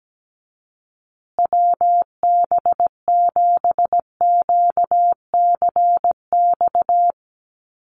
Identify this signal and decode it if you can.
Morse code, heard as a keyed tone, about 17 wpm, 700 Hz. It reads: WB7QCX
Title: USA Call Signs Repeated 2x 17wpm